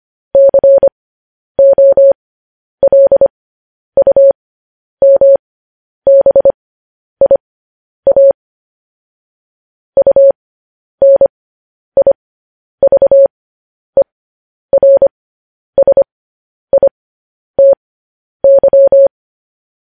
Morse code, heard as a keyed tone, about 25 wpm, spaced as for 10 wpm, 550 Hz. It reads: COLUMBIA UNIVERSITY
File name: podcast / News headlines-1x-25x10